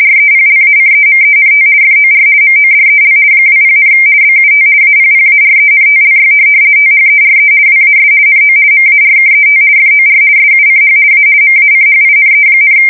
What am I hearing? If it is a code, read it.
RTTY, about 50 baud. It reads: REPLACEMENT FOR MODEL 20, BUT APPARENTLY THERE WAS NO MARKET FOR SUCH A MACHINE,
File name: fsk170-50B / teletype-corp